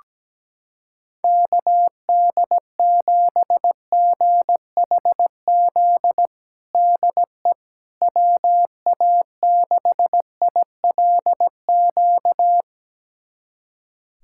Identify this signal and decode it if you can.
Morse code, heard as a keyed tone, about 17 wpm, 700 Hz. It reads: KD7GHZ DE WA6ILQ
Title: USA Callsign DE Callsign - 6 Characters - 17wpm